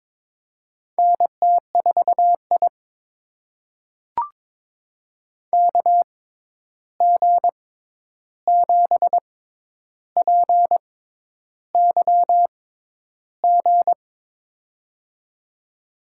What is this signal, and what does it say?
Morse code, heard as a keyed tone, about 22 wpm, spaced as for 10 wpm, 700 Hz. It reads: NT4I KG7PYG
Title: USA Call Signs - 6x Character Spacing - 22wpm